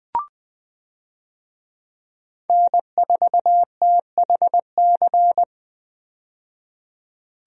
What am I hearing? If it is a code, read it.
Morse code, heard as a keyed tone, about 20 wpm, 700 Hz. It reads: N4THC
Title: USA Call Signs - 5 Characters - 20wpm